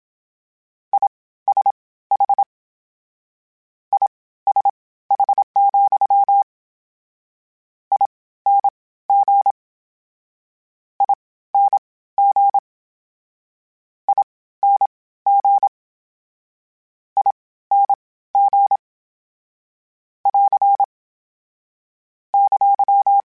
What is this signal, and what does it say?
Morse code, heard as a keyed tone, about 27 wpm, spaced as for 16 wpm, 800 Hz. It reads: ISH ISH, ING ING ING ING <AR> !
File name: General / General-mp3s / Suffixes sp @25